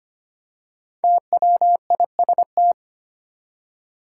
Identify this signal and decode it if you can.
Morse code, heard as a keyed tone, about 25 wpm, 700 Hz. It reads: TWIST
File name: 5 Letter Words - First 3 Letters Given - 25wpm